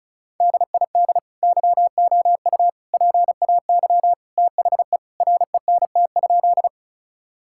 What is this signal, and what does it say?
Morse code, heard as a keyed tone, about 36 wpm, 700 Hz. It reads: DID YOU PAY THE RENT?
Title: Sentences from Top 1000 Words - Review Part 5 - 35wpm